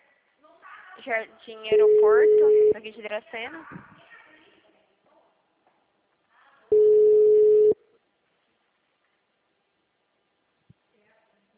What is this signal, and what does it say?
Ringback tone (Europe, 425 Hz), with speech.
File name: out- -2005- - - .69391